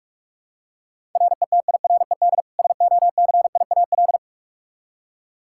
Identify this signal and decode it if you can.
Morse code, heard as a keyed tone, about 47 wpm, 700 Hz. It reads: RETIRED SOCIAL
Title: Practice QSO - Exchange 3 - More Info - 45wpm